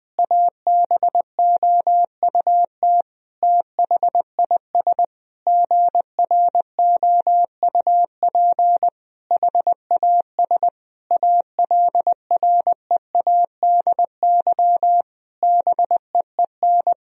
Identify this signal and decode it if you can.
Morse code, heard as a keyed tone, about 19 wpm, 700 Hz. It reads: ABOUT THIS GROUP HAS ALREADY BEEN
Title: Sentences from Top 1000 Words - Review Part 1 - 20wpm